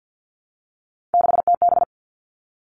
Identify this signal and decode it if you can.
Morse code, heard as a keyed tone, about 50 wpm, 700 Hz. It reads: 6TB
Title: All Cut Numbers - 3-Digit 50wpm